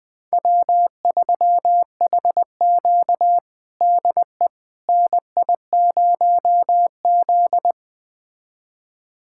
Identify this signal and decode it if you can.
Morse code, heard as a keyed tone, about 20 wpm, 700 Hz. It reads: W3HQ DE NI0Z
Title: USA Callsign DE Callsign - 4 Characters - 20wpm